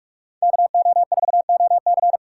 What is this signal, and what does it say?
Morse code, heard as a keyed tone, about 46 wpm, 700 Hz. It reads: KO4OC
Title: USA Call Signs - 6x Character Spacing - 45wpm